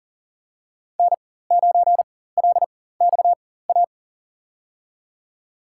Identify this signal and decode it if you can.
Morse code, heard as a keyed tone, about 40 wpm, spaced as for 19 wpm, 700 Hz. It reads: N9LXA
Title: USA Call Signs - 5 Characters - 4x Character Spacing - 40wpm